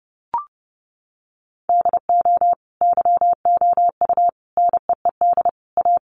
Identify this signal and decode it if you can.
Morse code, heard as a keyed tone, about 30 wpm, 700 Hz. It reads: DO YOU NEED A
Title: Sentences from Top 1500 Words 30wpm